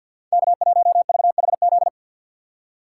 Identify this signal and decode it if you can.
Morse code, heard as a keyed tone, about 53 wpm, 700 Hz. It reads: K1VHZ
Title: USA Call Signs - 3x Character Spacing - 50wpm